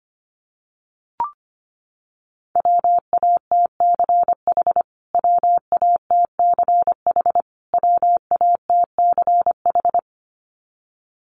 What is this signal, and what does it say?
Morse code, heard as a keyed tone, about 25 wpm, 700 Hz. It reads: WATCH WATCH WATCH
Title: Top 500 Words Repeated 25wpm